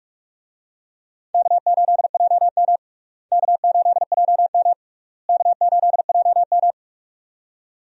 Morse code, heard as a keyed tone, about 46 wpm, 700 Hz. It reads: K8JM K8JM K8JM
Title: USA Call Signs - 4 Characters - Repeated 3x - 45wpm